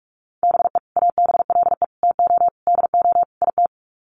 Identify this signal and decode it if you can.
Morse code, heard as a keyed tone, about 44 wpm, 700 Hz. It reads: BE ABLE TO DO IT
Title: Sets of 5 Words Encore2 45wpm